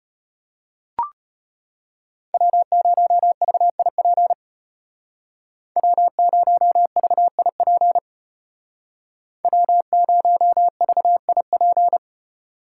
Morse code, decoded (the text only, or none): W0VIP W0VIP W0VIP